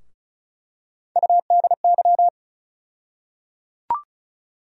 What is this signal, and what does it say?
Morse code, heard as a keyed tone, about 36 wpm, 700 Hz. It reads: UDY
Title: USA Call Sign Postfixes - 3 Letters 35wpm